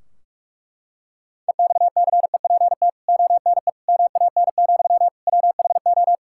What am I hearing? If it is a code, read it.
Morse code, heard as a keyed tone, about 41 wpm, 700 Hz. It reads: EXCEPT ONE MAN, WHO
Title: Sentences from Top 1600 Words 45wpm